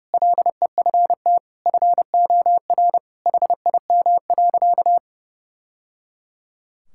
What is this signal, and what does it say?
Morse code, heard as a keyed tone, about 28 wpm, 700 Hz. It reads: LEFT FOR HIM.
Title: Sentences from Top 1000 Words - Review Part 6 - 30wpm